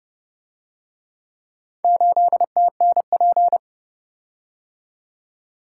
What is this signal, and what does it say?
Morse code, heard as a keyed tone, about 29 wpm, 700 Hz. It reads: 8TNP